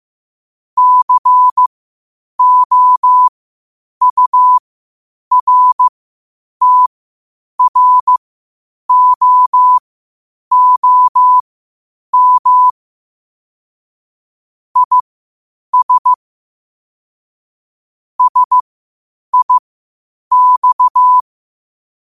Morse code, heard as a keyed tone, about 15 wpm, spaced as for 8 wpm, 1000 Hz. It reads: COURTROOM IS SIX
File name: qotdfiles / 8wpm